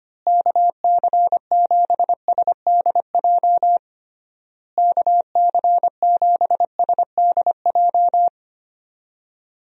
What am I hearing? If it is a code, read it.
Morse code, heard as a keyed tone, about 25 wpm, 700 Hz. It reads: KC7SDJ KC7SDJ
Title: USA Call Signs - 6 Characters - Repeated 3x - 25wpm